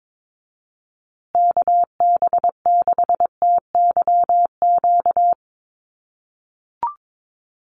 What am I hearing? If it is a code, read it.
Morse code, heard as a keyed tone, about 22 wpm, 700 Hz. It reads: KB6TYQ
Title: USA Call Signs Repeated 2x 22wpm